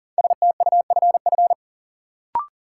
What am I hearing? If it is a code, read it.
Morse code, heard as a keyed tone, about 40 wpm, 700 Hz. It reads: STUFF
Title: Sentences from Top 1400 Words - New Word Drill Alt - 40wpm